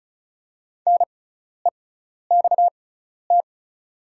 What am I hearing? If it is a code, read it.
Morse code, heard as a keyed tone, about 36 wpm, spaced as for 12 wpm, 700 Hz. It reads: NEXT
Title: 6x Character Spacing - Top 300 Words - 35wpm